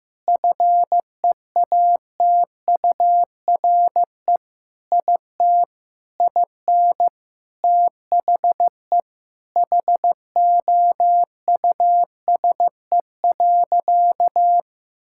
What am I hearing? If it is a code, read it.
Morse code, heard as a keyed tone, about 15 wpm, 700 Hz. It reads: FEATURE IT IN THE HOUSE.
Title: Sentences from Top 1000 Words - Review Part 5 - 15wpm